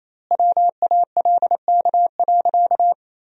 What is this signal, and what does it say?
Morse code, heard as a keyed tone, about 28 wpm, 700 Hz. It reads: WALK.
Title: 4x Word-Spacing - Sentences from Top 300 Words - 28wpm